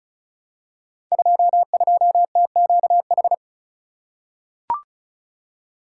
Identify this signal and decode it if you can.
Morse code, heard as a keyed tone, about 36 wpm, 700 Hz. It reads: 22TQH
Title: Code Groups - 5 Letter:Numbers - 35wpm